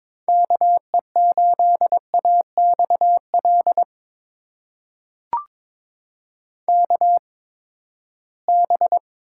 Morse code, heard as a keyed tone, about 22 wpm, spaced as for 15 wpm, 700 Hz. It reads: KE8AXL KB
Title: USA Call Signs - 8x Character Spacing - 22wpm